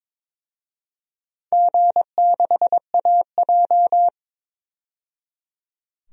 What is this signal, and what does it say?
Morse code, heard as a keyed tone, about 22 wpm, 700 Hz. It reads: G6AJ
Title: DXCC-100 Call Signs - 4 Characters - 22wpm